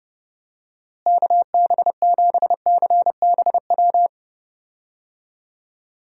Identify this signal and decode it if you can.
Morse code, heard as a keyed tone, about 29 wpm, 700 Hz. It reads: KB7CBW
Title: USA Call Signs - 6 Characters - Repeated 3x - 30wpm